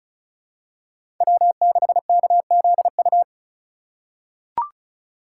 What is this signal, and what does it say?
Morse code, heard as a keyed tone, about 36 wpm, 700 Hz. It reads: W6KZU
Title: USA Call Signs - 5 Characters - 4x Character Spacing - 35wpm